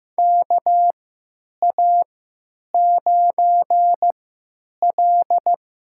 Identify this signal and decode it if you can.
Morse code, heard as a keyed tone, about 15 wpm, spaced as for 9 wpm, 700 Hz. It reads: KA9L
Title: USA Call Signs - 6 Characters - 3x Character Spacing - 15wpm